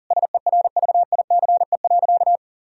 Morse code, heard as a keyed tone, about 37 wpm, 700 Hz. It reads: SERVICE.
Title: Sentences from Top 1000 Words - Review Part 4 - 40wpm